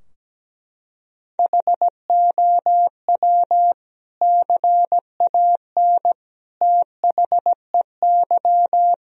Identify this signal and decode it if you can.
Morse code, heard as a keyed tone, about 17 wpm, 700 Hz. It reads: HOW CAN THEY
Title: Sentences from Top 1000 Words - Review Part 3 - 17wpm